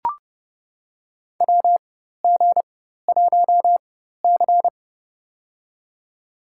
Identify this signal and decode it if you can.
Morse code, heard as a keyed tone, about 30 wpm, spaced as for 14 wpm, 700 Hz. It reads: WG1C
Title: USA Call Signs - 4 Characters - 4x Character Spacing - 30wpm